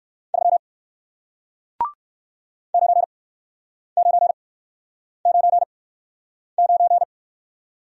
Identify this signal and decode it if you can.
Morse code, heard as a keyed tone, about 55 wpm, spaced as for 9 wpm, 700 Hz. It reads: 3 9999
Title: Speed-Racing Single Letter-Number 45wpm